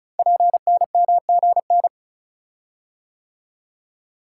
Morse code, heard as a keyed tone, about 36 wpm, 700 Hz. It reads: PNMGN
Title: Code Groups - 5 Letters - 35wpm